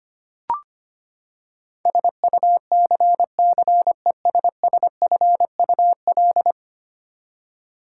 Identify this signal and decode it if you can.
Morse code, heard as a keyed tone, about 25 wpm, 700 Hz. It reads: SUCCESSFUL
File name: Ten Letter Words - 25wpm